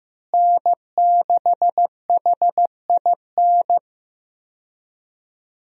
Morse code, heard as a keyed tone, about 15 wpm, 700 Hz. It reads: N6HIN
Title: USA Call Signs - 5 Characters - Repeated 2x - 15wpm